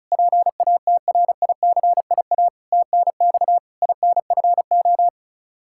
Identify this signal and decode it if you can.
Morse code, heard as a keyed tone, about 36 wpm, 700 Hz. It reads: PATRICIA TNX INFO
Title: Practice QSO - Exchange 3 - More Info - 35wpm